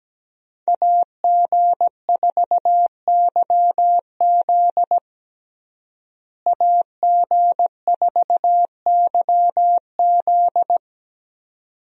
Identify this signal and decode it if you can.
Morse code, heard as a keyed tone, about 17 wpm, 700 Hz. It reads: AG4YZ AG4YZ
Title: USA Call Signs - 5 Characters - Repeated 2x - 17wpm